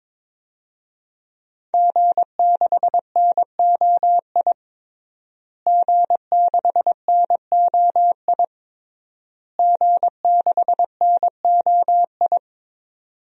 Morse code, heard as a keyed tone, about 22 wpm, 700 Hz. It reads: G6NOI G6NOI G6NOI
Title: DXCC-100 Call Signs - 5 Characters - Repeated 3x - 22wpm